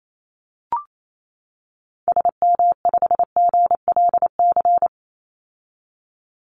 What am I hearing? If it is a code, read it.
Morse code, heard as a keyed tone, about 28 wpm, 700 Hz. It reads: SM5GLC
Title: Call Signs - 28wpm